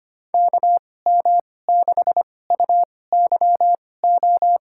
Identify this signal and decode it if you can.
Morse code, heard as a keyed tone, about 25 wpm, spaced as for 18 wpm, 700 Hz. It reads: KM6UYO
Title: USA Call Signs - 6 Characters - 2x Character Spacing - 25wpm